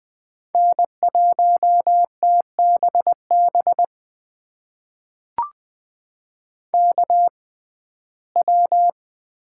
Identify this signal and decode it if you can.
Morse code, heard as a keyed tone, about 20 wpm, spaced as for 15 wpm, 700 Hz. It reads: N1TBB KW
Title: USA Call Signs - 5 Characters - 6x Character Spacing - 20wpm